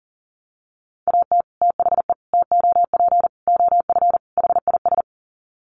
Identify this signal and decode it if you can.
Morse code, heard as a keyed tone, about 40 wpm, 700 Hz. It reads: AT THE TOP OF HIS
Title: Sets of 5 Words Encore 40wpm